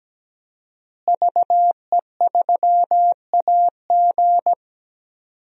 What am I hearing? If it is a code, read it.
Morse code, heard as a keyed tone, about 17 wpm, 700 Hz. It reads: VE3AG
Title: DXCC-100 Call Signs - 5 Characters - Repeated 2x - 17wpm